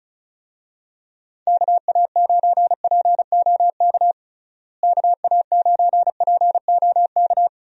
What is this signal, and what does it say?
Morse code, heard as a keyed tone, about 34 wpm, 700 Hz. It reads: KA9POK KA9POK
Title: USA Call Signs - 6 Characters - Repeated 3x - 35wpm